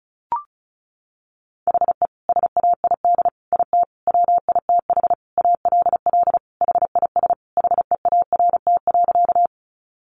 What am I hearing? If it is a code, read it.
Morse code, heard as a keyed tone, about 34 wpm, 700 Hz. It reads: HE SAID IT WITH ALL HIS HEART.
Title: Sentences from Top 1200 Words 35wpm